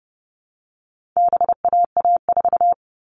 Morse code, heard as a keyed tone, about 30 wpm, 700 Hz. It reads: BAA4
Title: All Cut Numbers - 4-Digit 30wpm